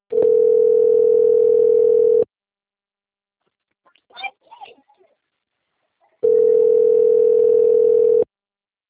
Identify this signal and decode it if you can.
Ringback tone (North America), 440 + 480 Hz.
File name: rg-1600-unknown- - - .43589